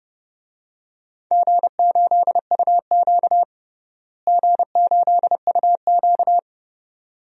Morse code, heard as a keyed tone, about 32 wpm, 700 Hz. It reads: G8UQ G8UQ
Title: DXCC-100 Call Signs - 4 Characters - Repeated 2x - 30wpm